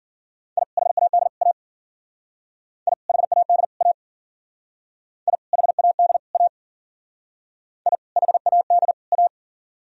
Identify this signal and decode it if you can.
Morse code, heard as a keyed tone, about 50 wpm, 700 Hz. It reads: I HAD A I HAD A I HAD A I HAD A
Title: Speed-Racing - Sets of 3 Words 40wpm